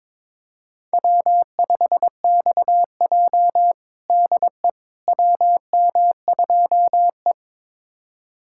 Morse code, heard as a keyed tone, about 22 wpm, 700 Hz. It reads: W5XJ DE WM2E
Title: USA Callsign DE Callsign - 4 Characters - 22wpm